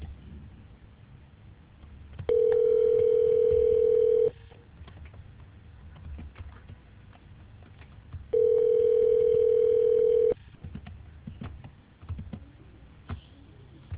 Ringback tone (North America), 440 + 480 Hz.